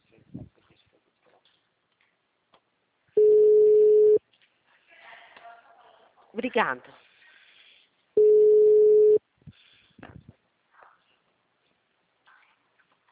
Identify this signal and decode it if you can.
Ringback tone (Europe, 425 Hz), with speech.
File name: out- -2010- - - .73512